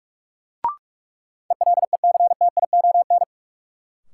Morse code, heard as a keyed tone, about 46 wpm, 700 Hz. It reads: ELECTION
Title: Sentences from Top 1500 Words - New Word Drill Alt - 45wpm